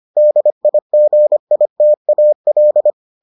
Morse code, heard as a keyed tone, about 25 wpm, 600 Hz. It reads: DIGITAL